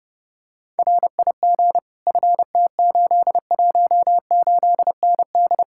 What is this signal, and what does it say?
Morse code, heard as a keyed tone, about 30 wpm, 700 Hz. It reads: RIG FT818ND
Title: Speed-Racing Common QSO Phrases - 30wpm